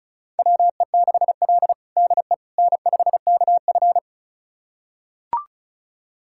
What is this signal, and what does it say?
Morse code, heard as a keyed tone, about 36 wpm, 700 Hz. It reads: WE6L DE N5KF
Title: USA Callsign DE Callsign - 4 Characters - 35wpm